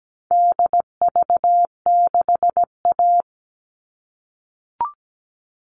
Morse code, heard as a keyed tone, about 17 wpm, 700 Hz. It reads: DV6A
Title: All Cut Numbers - 4-Digit 17wpm